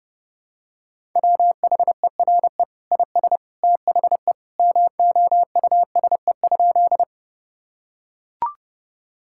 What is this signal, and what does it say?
Morse code, heard as a keyed tone, about 32 wpm, 700 Hz. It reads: WHERE IS THE MOUSE?
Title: Sentences from Top 1000 Words - Review Part 5 - 30wpm